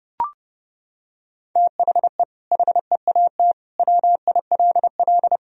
Morse code, heard as a keyed tone, about 32 wpm, 700 Hz. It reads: THE HEAT WILL
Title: Sentences from Top 1000 Words - Review Part 3 - 30wpm